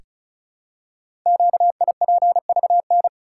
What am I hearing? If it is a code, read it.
Morse code, heard as a keyed tone, about 36 wpm, 700 Hz. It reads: QIPVN
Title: Code Groups - 5 Letter:Numbers - 35wpm